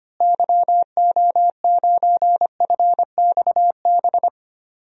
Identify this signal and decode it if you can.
Morse code, heard as a keyed tone, about 25 wpm, 700 Hz. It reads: YO9FXB
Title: DXCC-100 Call Signs - 6 Characters - Repeated 3x - 25wpm